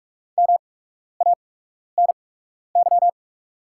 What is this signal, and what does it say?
Morse code, heard as a keyed tone, about 44 wpm, spaced as for 12 wpm, 690 Hz. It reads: MANY
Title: 8x Character Spacing - Top 100 Words - 45wpm